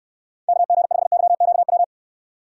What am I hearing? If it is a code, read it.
Morse code, heard as a keyed tone, about 69 wpm, 700 Hz. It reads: KG5YQW